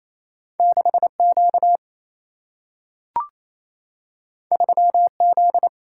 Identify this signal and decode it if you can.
Morse code, heard as a keyed tone, about 28 wpm, 700 Hz. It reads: 6Q 3Z
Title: Single-Letter Single-Number 28wpm